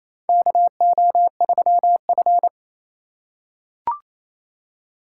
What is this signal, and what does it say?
Morse code, heard as a keyed tone, about 28 wpm, 700 Hz. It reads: KO3F
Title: USA Call Signs - 4 Characters - 8x Character Spacing - 28wpm